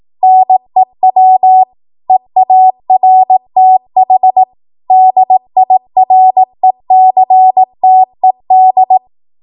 Morse code, heard as a keyed tone, about 18 wpm, 750 Hz. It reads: NEW EARTH DIRECTED
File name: Archive / 18 WPM / 18WPM